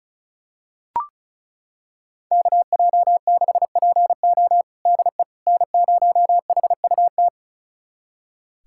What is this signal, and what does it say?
Morse code, heard as a keyed tone, about 36 wpm, 700 Hz. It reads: KJ6PO DE N0HUT
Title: USA Callsign DE Callsign - 5 Characters - 35wpm